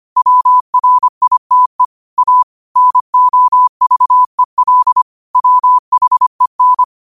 Morse code, heard as a keyed tone, about 25 wpm, 1000 Hz. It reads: WRITE A NOVEL WHEN
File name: 25wpm